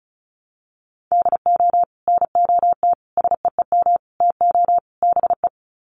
Morse code, heard as a keyed tone, about 34 wpm, 700 Hz. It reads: DO NOT SEEM TO BE
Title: Sets of 5 Words Encore2 35wpm